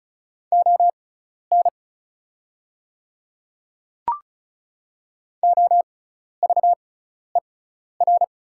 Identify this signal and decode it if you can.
Morse code, heard as a keyed tone, about 36 wpm, spaced as for 12 wpm, 700 Hz. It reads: ON OVER
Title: 6x Character Spacing - Top 100 Words - 35wpm